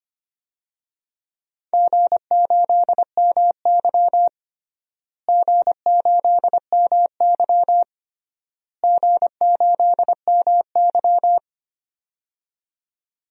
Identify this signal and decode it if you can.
Morse code, heard as a keyed tone, about 25 wpm, 700 Hz. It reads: G8MY G8MY G8MY
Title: DXCC-100 Call Signs - 4 Characters - Repeated 3x - 25wpm